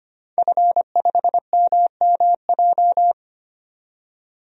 Morse code, heard as a keyed tone, about 25 wpm, 700 Hz. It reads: F5MMJ
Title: DXCC-100 Call Signs - 5 Characters - 25wpm